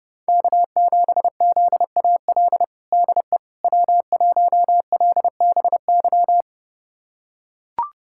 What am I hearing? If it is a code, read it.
Morse code, heard as a keyed tone, about 30 wpm, 700 Hz. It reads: K7ZAL DE W1LBY
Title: USA Callsign DE Callsign - 5 Characters - 30wpm